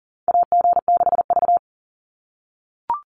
Morse code, heard as a keyed tone, about 40 wpm, 700 Hz. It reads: AG6V
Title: NA SOTA Chaser Callsigns 40wpm